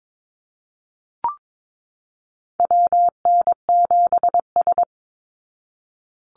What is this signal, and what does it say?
Morse code, heard as a keyed tone, about 22 wpm, 700 Hz. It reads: WN7S
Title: Call Signs Repeated 2x - 22wpm